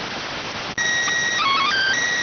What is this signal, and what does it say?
SSTV (slow-scan television), PD 290.